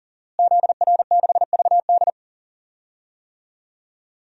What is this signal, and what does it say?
Morse code, heard as a keyed tone, about 40 wpm, 700 Hz. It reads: ZR6VD
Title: DXCC-100 Call Signs - 5 Characters - Repeated 2x - 40wpm